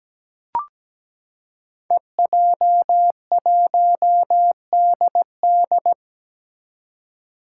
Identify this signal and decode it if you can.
Morse code, heard as a keyed tone, about 17 wpm, 700 Hz. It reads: EJ1DD
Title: Five Challenging Letter-Number Combinations 17wpm